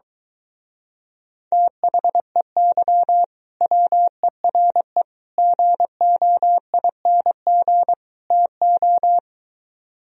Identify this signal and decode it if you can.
Morse code, heard as a keyed tone, about 22 wpm, 700 Hz. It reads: THEY WERE GOING TO